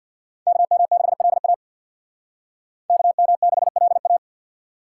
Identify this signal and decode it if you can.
Morse code, heard as a keyed tone, about 55 wpm, 700 Hz. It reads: KM6LA KM6LA
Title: Speed Racing - USA 2x2 Call Signs 50wpm